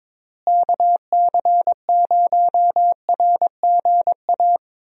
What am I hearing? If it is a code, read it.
Morse code, heard as a keyed tone, about 21 wpm, 700 Hz. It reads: KC0RGA
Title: USA Call Signs - 6 Characters - Repeated 2x - 22wpm